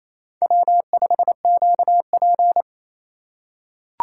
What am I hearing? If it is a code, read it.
Morse code, heard as a keyed tone, about 28 wpm, 700 Hz. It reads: W5QP
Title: K1USN SST QSO Practice - 28wpm